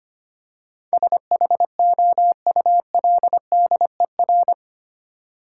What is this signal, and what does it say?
Morse code, heard as a keyed tone, about 25 wpm, 700 Hz. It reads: SHOULDER
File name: 8 Letter Words - First 3 Letters Given - 25wpm